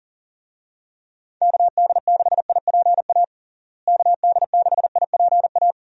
Morse code, heard as a keyed tone, about 40 wpm, 700 Hz. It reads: KD6IPA KD6IPA
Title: USA Call Signs - 6 Characters - Repeated 2x - 40wpm